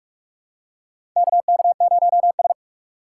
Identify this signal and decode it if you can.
Morse code, heard as a keyed tone, about 46 wpm, 700 Hz. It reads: KK0S
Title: USA Call Signs - 4 Characters - 4x Character Spacing - 45wpm